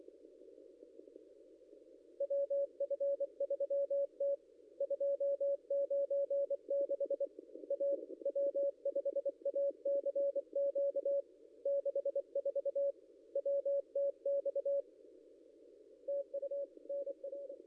Morse code, heard as a keyed tone, about 23 wpm, 560 Hz. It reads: WF3T 296 A W5ACQ 64 WTX TU NR
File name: contests / ss-cw-k5tr-op-n5rz -0800